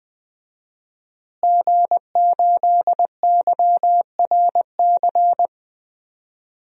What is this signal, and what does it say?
Morse code, heard as a keyed tone, about 20 wpm, 700 Hz. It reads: G8YRC
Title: DXCC-100 Call Signs - 5 Characters - Repeated 2x - 20wpm